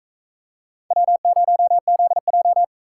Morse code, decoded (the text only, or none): W0ZJ